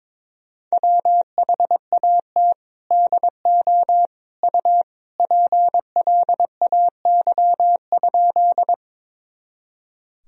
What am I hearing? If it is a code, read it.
Morse code, heard as a keyed tone, about 22 wpm, 700 Hz. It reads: WHAT DO U PLAY?
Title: Sentences from Top 200 Words and Abbreviations 22wpm